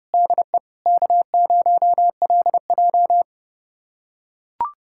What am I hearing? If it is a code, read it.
Morse code, heard as a keyed tone, about 29 wpm, 700 Hz. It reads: DE K0LJ
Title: USA Callsign DE Callsign - 4 Characters - 30wpm